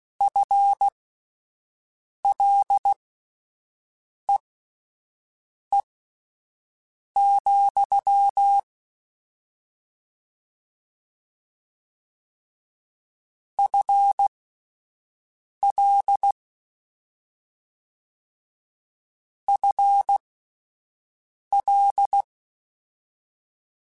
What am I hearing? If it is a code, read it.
Morse code, heard as a keyed tone, about 16 wpm, spaced as for 5 wpm, 800 Hz. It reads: FLEE, FL FL